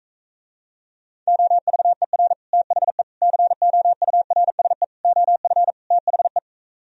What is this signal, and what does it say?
Morse code, heard as a keyed tone, about 45 wpm, 700 Hz. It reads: OVER THE COURSE OF THE